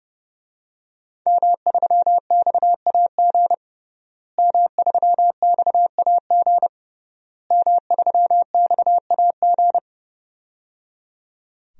Morse code, heard as a keyed tone, about 30 wpm, 700 Hz. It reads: M3XAG M3XAG M3XAG
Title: DXCC-100 Call Signs - 5 Characters - Repeated 3x - 30wpm